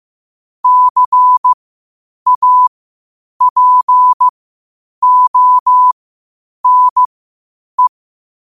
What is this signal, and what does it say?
Morse code, heard as a keyed tone, about 15 wpm, spaced as for 8 wpm, 1000 Hz. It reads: CAPONE
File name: qotdfiles / 8wpm